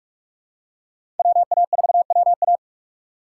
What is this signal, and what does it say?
Morse code, heard as a keyed tone, about 45 wpm, 690 Hz. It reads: WA4WA
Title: USA Call Signs - 5 Characters - 2x Character Spacing - 45wpm